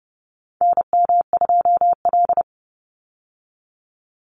Morse code, heard as a keyed tone, about 30 wpm, 700 Hz. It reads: NM2L
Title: Call Signs Repeated 3x - 30wpm